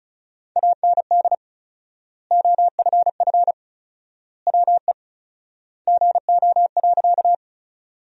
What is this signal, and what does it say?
Morse code, heard as a keyed tone, about 36 wpm, 700 Hz. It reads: AND OFF WE GO.
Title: 4x Word-Spacing - Sentences from Top 200 Words - 35wpm